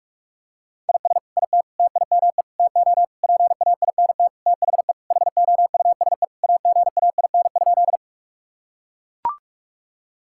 Morse code, heard as a keyed tone, about 41 wpm, 700 Hz. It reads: IS IT TIME TO PAINT THE HOUSE AGAIN?
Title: Sentences from Top 1000 Words - Review Part 6 - 45wpm